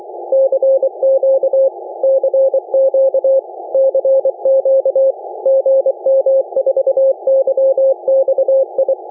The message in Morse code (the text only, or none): CQ CQ CQ GM4YXI